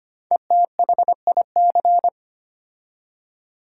Morse code, heard as a keyed tone, about 25 wpm, 700 Hz. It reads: ETHIC
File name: 5 Letter Words - First 3 Letters Given - 25wpm